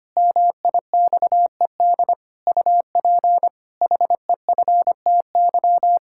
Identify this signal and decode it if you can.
Morse code, heard as a keyed tone, about 25 wpm, 700 Hz. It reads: MIXED UP HEFTY
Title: Pangram Challenge v2 - 25wpm